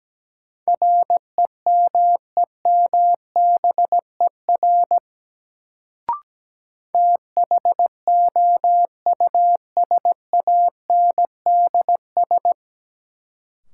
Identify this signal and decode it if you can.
Morse code, heard as a keyed tone, about 17 wpm, 700 Hz. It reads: REMEMBER THOUSANDS
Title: Sets of 4 Words - New Word Drill - 17wpm